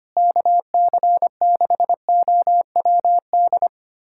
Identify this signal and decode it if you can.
Morse code, heard as a keyed tone, about 25 wpm, 700 Hz. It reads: KC6OWD
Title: USA Callsign DE Callsign - 6 Characters - 25wpm